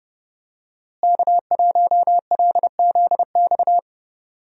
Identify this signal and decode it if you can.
Morse code, heard as a keyed tone, about 30 wpm, 700 Hz. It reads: K1LZX
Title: USA Call Signs - 5 Characters - Repeated 2x - 30wpm